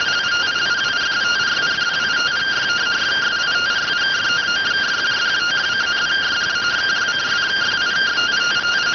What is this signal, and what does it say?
Two-tone teleprinter signal, about 170 Hz shift.